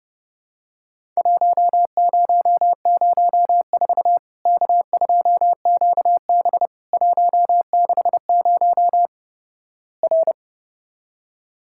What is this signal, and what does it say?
Morse code, heard as a keyed tone, about 32 wpm, 700 Hz. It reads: 1004 K2QB 160 R
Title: Worked All Europe DX Contest - QTC Exchange - 30wpm